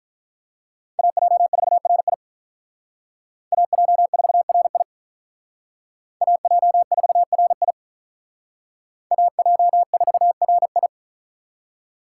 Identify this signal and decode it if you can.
Morse code, heard as a keyed tone, about 42 wpm, 700 Hz. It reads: AJ4RI AJ4RI AJ4RI AJ4RI
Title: Speed Racing - USA 2x2 Call Signs 35wpm